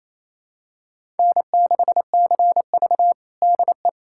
Morse code, heard as a keyed tone, about 28 wpm, 700 Hz. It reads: N6CV DE
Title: USA Callsign DE Callsign - 4 Characters - 28wpm